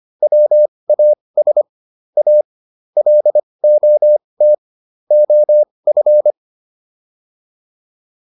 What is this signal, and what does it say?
Morse code, heard as a keyed tone, about 25 wpm, spaced as for 20 wpm, 600 Hz. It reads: WAS A LOT OF